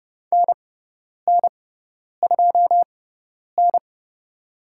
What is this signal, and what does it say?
Morse code, heard as a keyed tone, about 30 wpm, spaced as for 10 wpm, 700 Hz. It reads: NN2N
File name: Lesson 18g - Letter m - Call Signs 30wpm